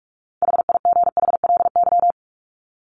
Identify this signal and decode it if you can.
Morse code, heard as a keyed tone, about 44 wpm, 700 Hz. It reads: HIGHLY
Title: Top 1001-1250 Words 45wpm